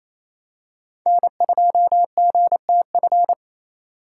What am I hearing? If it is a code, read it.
Morse code, heard as a keyed tone, about 28 wpm, 700 Hz. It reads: N2GTF
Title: USA Call Signs - 5 Characters - 4x Character Spacing - 28wpm